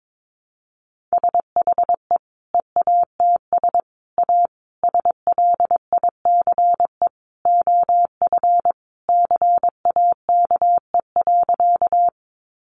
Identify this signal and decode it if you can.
Morse code, heard as a keyed tone, about 22 wpm, 700 Hz. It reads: SHE EATS A SLICE OF CAKE.
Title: Easy Phrases 22wpm